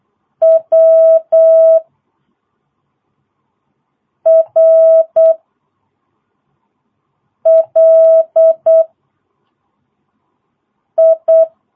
Morse code, heard as a keyed tone, about 8 wpm, 650 Hz. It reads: W R L I
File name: jh8cbh / 2022cw